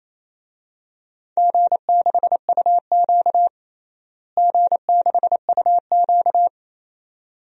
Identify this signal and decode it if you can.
Morse code, heard as a keyed tone, about 28 wpm, 700 Hz. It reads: G6UQ G6UQ
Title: DXCC-100 Call Signs - 4 Characters - Repeated 2x - 28wpm